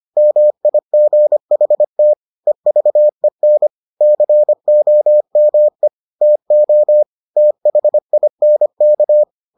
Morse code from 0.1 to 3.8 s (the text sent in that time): MIGHT EVEN